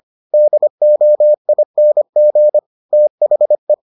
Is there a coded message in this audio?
Yes: Morse code from 0.3 to 3.7 s, heard as a keyed tone, about 25 wpm, 600 Hz. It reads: DOING THE